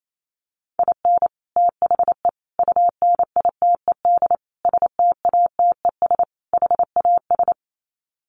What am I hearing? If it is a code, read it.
Morse code, heard as a keyed tone, about 28 wpm, 700 Hz. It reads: IN THE UNITED STATES HAS
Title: Sets of 5 Words Encore2 28wpm